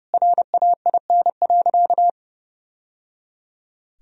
Morse code, heard as a keyed tone, about 30 wpm, 700 Hz. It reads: RAIN.
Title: Sentences from Top 1000 Words - Review Part 1 - 30wpm